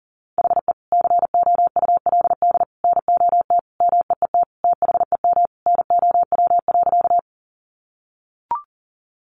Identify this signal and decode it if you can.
Morse code, heard as a keyed tone, about 40 wpm, 700 Hz. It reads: HE COULD NOT MEET THEM NOW.
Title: Sentences from Top 1100 Words 40wpm